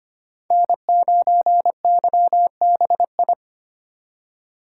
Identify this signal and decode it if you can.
Morse code, heard as a keyed tone, about 30 wpm, 700 Hz. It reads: N9YBI
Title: Speed Racing - USA 1x3 Call Signs 25wpm